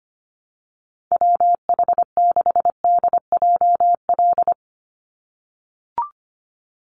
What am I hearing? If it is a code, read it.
Morse code, heard as a keyed tone, about 25 wpm, 700 Hz. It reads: WH6DJL
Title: USA Call Signs 25wpm